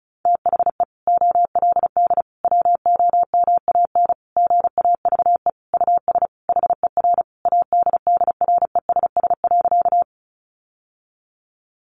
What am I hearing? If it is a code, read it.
Morse code, heard as a keyed tone, about 34 wpm, 700 Hz. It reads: THE OLD WOMAN GAVE US HER ADDRESS.
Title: Sentences from Top 1100 Words 35wpm